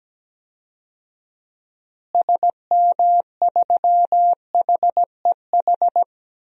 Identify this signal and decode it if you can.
Morse code, heard as a keyed tone, about 17 wpm, 700 Hz. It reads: SM3HEH
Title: DXCC-100 Call Signs - 6 Characters - Repeated 2x - 17wpm